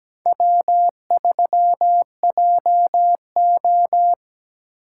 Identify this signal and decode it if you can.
Morse code, heard as a keyed tone, about 17 wpm, 700 Hz. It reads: W3JO
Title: USA Call Signs - 4 Characters - 4x Character Spacing - 17wpm